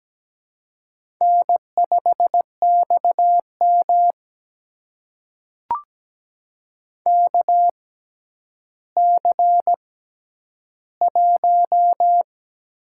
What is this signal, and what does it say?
Morse code, heard as a keyed tone, about 17 wpm, spaced as for 10 wpm, 700 Hz. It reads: N5XM KC1
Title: USA Call Signs - 4 Characters - 6x Character Spacing - 17wpm